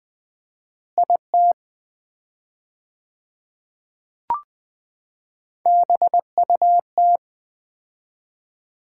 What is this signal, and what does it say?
Morse code, heard as a keyed tone, about 21 wpm, 700 Hz. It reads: IT BUT
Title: 4x Word-Spacing - Sets of 2 Words - 20wpm